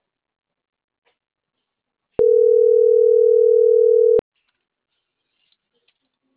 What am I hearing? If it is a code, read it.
Ringback tone (North America), 440 + 480 Hz.